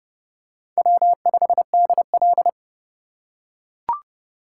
Morse code, heard as a keyed tone, about 28 wpm, 700 Hz. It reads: W5DL